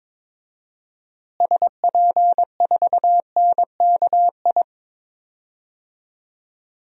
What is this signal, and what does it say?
Morse code, heard as a keyed tone, about 21 wpm, 700 Hz. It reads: SP4NKI
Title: DXCC-100 Call Signs - 6 Characters - Repeated 2x - 22wpm